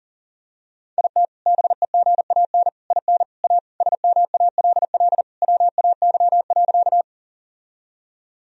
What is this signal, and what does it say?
Morse code, heard as a keyed tone, about 40 wpm, 700 Hz. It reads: IT BEGAN IN A SMALL WAY.
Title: Sentences from Top 1000 Words - Review Part 1 - 40wpm